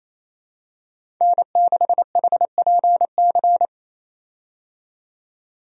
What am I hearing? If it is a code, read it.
Morse code, heard as a keyed tone, about 27 wpm, 700 Hz. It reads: N6HPC
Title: USA Call Signs - 5 Characters - 28wpm